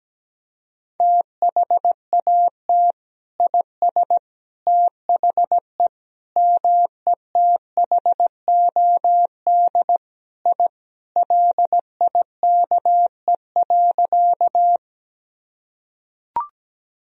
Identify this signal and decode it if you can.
Morse code, heard as a keyed tone, about 17 wpm, 700 Hz. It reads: THAT IS THE METHOD I LIKE.
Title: Sentences from Top 1000 Words - Review Part 3 - 17wpm